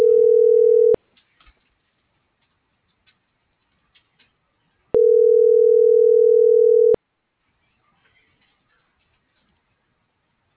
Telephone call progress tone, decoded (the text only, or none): ringback tone